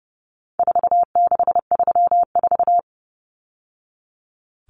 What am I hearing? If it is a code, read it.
Morse code, heard as a keyed tone, about 30 wpm, 700 Hz. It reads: 4634
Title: Numbers - 4-Digit 30wpm